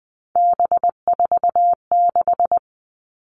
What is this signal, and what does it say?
Morse code, heard as a keyed tone, about 20 wpm, 700 Hz. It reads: B46
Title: All Cut Numbers - 3-Digit 20wpm